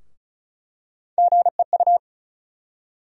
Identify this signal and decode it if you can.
Morse code, heard as a keyed tone, about 34 wpm, 700 Hz. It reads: GEU
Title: USA Call Sign Postfixes - 3 Letters 35wpm